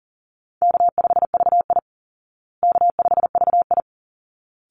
Morse code, heard as a keyed tone, about 40 wpm, 700 Hz. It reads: K5VI K5VI
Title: Easy Call Signs Repeated 2x - 40wpm